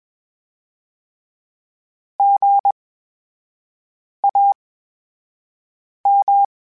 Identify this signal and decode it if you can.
Morse code, heard as a keyed tone, about 20 wpm, spaced as for 5 wpm, 800 Hz. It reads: GAM
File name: Special INGs-mp3s / Special INGs-mp3s / VerySlowToFaster Ing @20